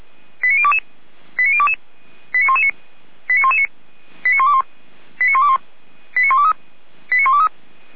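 Five-tone selective call (ZVEI), heard as quick runs of five tones, tones of about 70 ms. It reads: 89022 89022 89119 89119 89121 89121 89123 89123
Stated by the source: ZVEI five-tone call received by SDR.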